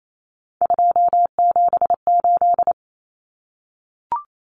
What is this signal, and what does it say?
Morse code, heard as a keyed tone, about 28 wpm, 700 Hz. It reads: 278
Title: Common Cut Numbers - 3-Digit 28wpm